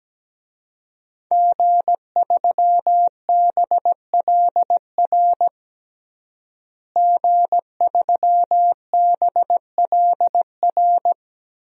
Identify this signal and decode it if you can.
Morse code, heard as a keyed tone, about 17 wpm, 700 Hz. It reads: G3BLR G3BLR
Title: DXCC-100 Call Signs - 5 Characters - Repeated 3x - 17wpm